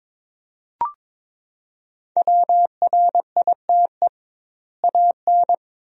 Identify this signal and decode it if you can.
Morse code, heard as a keyed tone, about 22 wpm, 700 Hz. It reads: WRITE AN
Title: 2x Word-Spacing - Sentences from Top 300 Words - 22wpm